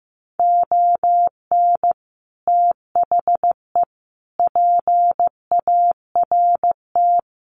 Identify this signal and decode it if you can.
Morse code, heard as a keyed tone, about 15 wpm, 700 Hz. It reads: ON THE PART
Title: Sets of 5 Words 15wpm